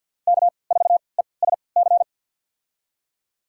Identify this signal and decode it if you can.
Morse code, heard as a keyed tone, about 48 wpm, spaced as for 28 wpm, 700 Hz. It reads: K4ESC
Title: USA Call Signs - 5 Characters - 3x Character Spacing - 50wpm